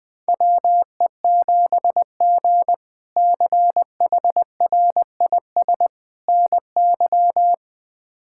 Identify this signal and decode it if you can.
Morse code, heard as a keyed tone, about 20 wpm, 700 Hz. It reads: WE7G CHRIS NY
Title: K1USN SST QSO Practice - 20wpm